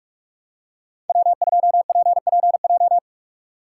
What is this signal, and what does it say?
Morse code, heard as a keyed tone, about 46 wpm, 700 Hz. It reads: W2PPJ
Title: USA Call Signs - 5 Characters - 2x Character Spacing - 45wpm